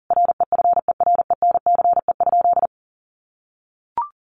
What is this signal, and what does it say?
Morse code, heard as a keyed tone, about 40 wpm, 700 Hz. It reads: REFERENCE?
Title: Sentences from Top 1200 Words 40wpm